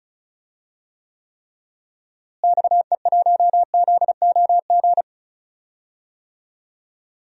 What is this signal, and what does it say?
Morse code, heard as a keyed tone, about 36 wpm, 700 Hz. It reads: XE1ZOG
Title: DXCC-100 Call Signs - 6 Characters - Repeated 3x - 35wpm